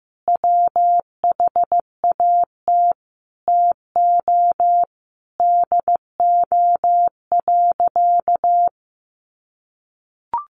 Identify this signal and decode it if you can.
Morse code, heard as a keyed tone, about 15 wpm, 700 Hz. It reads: WHAT TO DO.
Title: Sentences from Top 1400 Words 15wpm